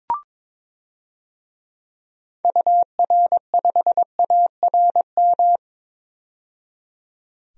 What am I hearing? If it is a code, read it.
Morse code, heard as a keyed tone, about 22 wpm, 700 Hz. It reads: UR5ARM
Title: DXCC-100 Call Signs - 6 Characters - 22wpm